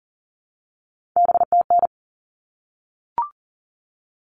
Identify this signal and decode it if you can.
Morse code, heard as a keyed tone, about 40 wpm, 700 Hz. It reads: BTN
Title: All Cut Numbers - 3-Digit 40wpm